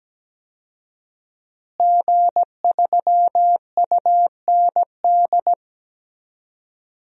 Morse code, heard as a keyed tone, about 17 wpm, 700 Hz. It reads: G3UND
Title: DXCC-100 Call Signs - 5 Characters - Repeated 3x - 17wpm